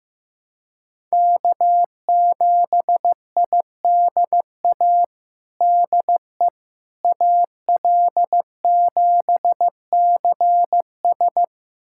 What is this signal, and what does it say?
Morse code, heard as a keyed tone, about 15 wpm, 700 Hz. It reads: K7IDA DE AL7CS
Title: USA Callsign DE Callsign - 5 Characters - 15wpm